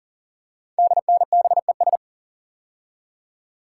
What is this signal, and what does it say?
Morse code, heard as a keyed tone, about 40 wpm, 700 Hz. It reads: DNBES
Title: Code Groups - 5 Letters - 40wpm